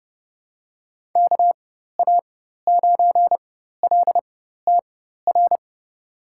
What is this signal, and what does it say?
Morse code, heard as a keyed tone, about 30 wpm, spaced as for 14 wpm, 700 Hz. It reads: KA9LTR
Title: USA Call Signs - 6 Characters - 4x Character Spacing - 30wpm